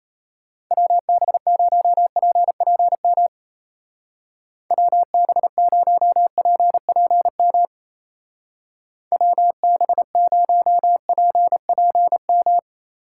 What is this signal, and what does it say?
Morse code, heard as a keyed tone, about 34 wpm, 700 Hz. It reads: WB0PPM WB0PPM WB0PPM
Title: Speed Racing - USA Call Signs 28wpm